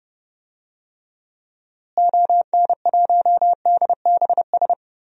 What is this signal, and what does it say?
Morse code, heard as a keyed tone, about 31 wpm, 700 Hz. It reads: ON1DBS
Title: DXCC-100 Call Signs - 6 Characters - Repeated 2x - 30wpm